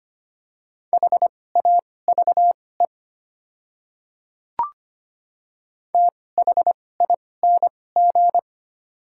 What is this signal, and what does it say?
Morse code, heard as a keyed tone, about 25 wpm, spaced as for 18 wpm, 700 Hz. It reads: HAVE THING
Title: 2x Character Spacing - Top 100 Words - 25wpm